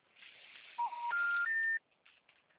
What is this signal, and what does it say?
Special information tone.